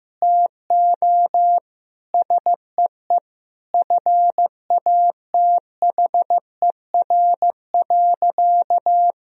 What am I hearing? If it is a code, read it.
Morse code, heard as a keyed tone, about 16 wpm, 700 Hz. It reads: TO SEE FATHER.
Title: Sentences from Top 1000 Words - Review Part 1 - 15wpm